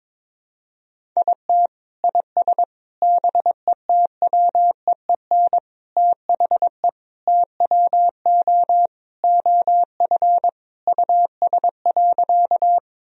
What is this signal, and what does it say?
Morse code, heard as a keyed tone, about 22 wpm, 700 Hz. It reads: IT IS BETWEEN THE TWO OF US.
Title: Sentences from Top 1000 Words - Review Part 1 - 22wpm